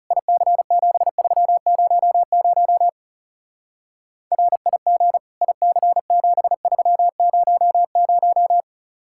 Morse code, heard as a keyed tone, about 35 wpm, 700 Hz. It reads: IC7300 RIG IC7300
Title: Speed-Racing Common QSO Phrases - 30wpm